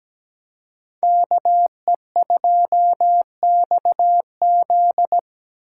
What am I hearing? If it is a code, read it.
Morse code, heard as a keyed tone, about 17 wpm, 700 Hz. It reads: KE2XZ
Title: USA Call Signs - 4x Character Spacing - 17wpm